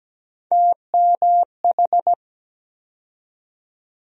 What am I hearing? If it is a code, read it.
Morse code, heard as a keyed tone, about 17 wpm, 700 Hz. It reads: TMH
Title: USA Call Sign Postfixes - 3 Letters 17wpm